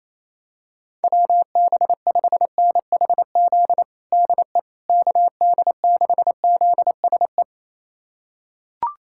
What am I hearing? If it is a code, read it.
Morse code, heard as a keyed tone, about 28 wpm, 700 Hz. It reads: WB5NHZ DE KD6ZSE
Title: USA Callsign DE Callsign 28wpm